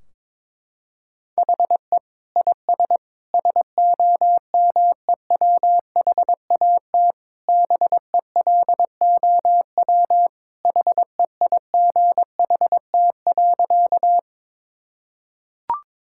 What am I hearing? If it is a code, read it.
Morse code, heard as a keyed tone, about 22 wpm, 700 Hz. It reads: HE IS SOMEWHAT BELOW HEIGHT.
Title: Sentences from Top 1600 Words 22wpm